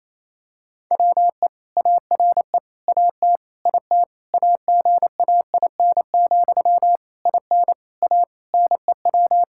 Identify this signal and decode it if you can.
Morse code, heard as a keyed tone, about 28 wpm, 700 Hz. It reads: WE ARE AT IT AGAIN, IN A NEW
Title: Sentences from Top 1000 Words - Review Part 5 - 28wpm